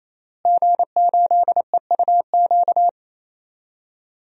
Morse code, heard as a keyed tone, about 27 wpm, 700 Hz. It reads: G8EUQ
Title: DXCC-100 Call Signs - 5 Characters - Repeated 3x - 28wpm